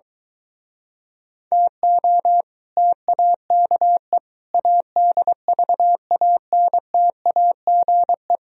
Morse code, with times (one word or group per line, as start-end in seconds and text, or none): 1.5-2.4 TO
2.8-4.2 TAKE
4.5-8.4 ADVANTAGE